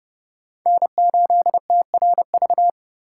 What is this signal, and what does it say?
Morse code, heard as a keyed tone, about 30 wpm, 700 Hz. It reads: N8TRV
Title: USA Call Signs - 5 Characters - 3x Character Spacing - 30wpm